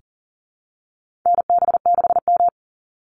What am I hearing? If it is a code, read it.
Morse code, heard as a keyed tone, about 40 wpm, 700 Hz. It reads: NB6M
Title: Call Signs Repeated 3x - 40wpm